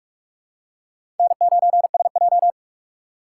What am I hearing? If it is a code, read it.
Morse code, heard as a keyed tone, about 46 wpm, 700 Hz. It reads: N9SJ
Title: USA Call Signs - 4 Characters - 6x Character Spacing - 45wpm